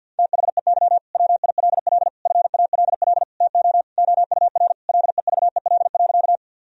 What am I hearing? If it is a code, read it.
Morse code, heard as a keyed tone, about 50 wpm, 700 Hz. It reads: THEY WILL FALL TO OUR LEVEL.
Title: Sentences from Top 1000 Words - Review Part 4 - 50wpm